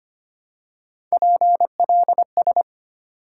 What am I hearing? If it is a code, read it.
Morse code, heard as a keyed tone, about 25 wpm, 700 Hz. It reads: PLS
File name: Top 90 QSO Elements - 3x Character Spacing - 25wpm